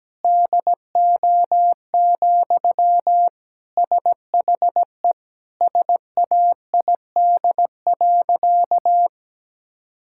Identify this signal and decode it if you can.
Morse code, heard as a keyed tone, about 17 wpm, 700 Hz. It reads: DO, SHE SAID.
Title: Sentences from Top 1000 Words - Review Part 5 - 17wpm